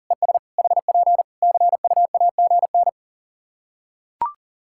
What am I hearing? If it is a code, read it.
Morse code, heard as a keyed tone, about 40 wpm, 700 Hz. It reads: ES HP CUAGN
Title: Practice QSO - Exchange 4 - Ending - 40wpm